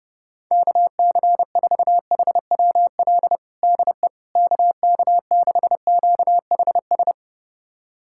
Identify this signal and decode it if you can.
Morse code, heard as a keyed tone, about 30 wpm, 700 Hz. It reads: KC4HWL DE KK6QHS
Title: USA Callsign DE Callsign 30wpm